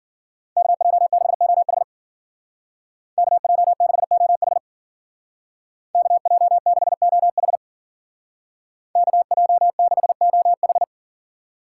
Morse code, heard as a keyed tone, about 52 wpm, 700 Hz. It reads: KJ6OH KJ6OH KJ6OH KJ6OH